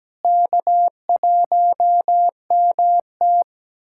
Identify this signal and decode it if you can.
Morse code, heard as a keyed tone, about 17 wpm, 700 Hz. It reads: K1MT
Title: USA Callsign DE Callsign - 4 Characters - 17wpm